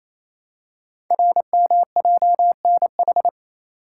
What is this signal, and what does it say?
Morse code, heard as a keyed tone, about 27 wpm, 700 Hz. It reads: RMJNH
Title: Code Groups - 5 Letters - 28wpm